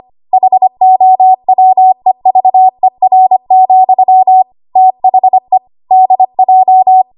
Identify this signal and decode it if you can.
Morse code, heard as a keyed tone, about 25 wpm, 750 Hz. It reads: HOWEVER, THE DJ
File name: Archive / 25 WPM / 25WPM